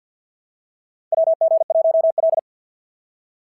Morse code, heard as a keyed tone, about 51 wpm, 690 Hz. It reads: WG1L
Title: SOTA QSO Practice 50wpm